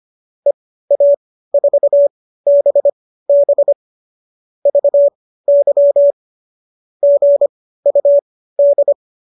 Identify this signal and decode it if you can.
Morse code, heard as a keyed tone, about 25 wpm, spaced as for 15 wpm, 570 Hz. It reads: EA4BB VY GUD